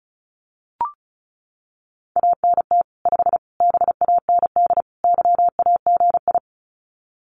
Morse code, heard as a keyed tone, about 34 wpm, 700 Hz. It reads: ANT 5 BAND YAGI
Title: Common QSO Phrases 35wpm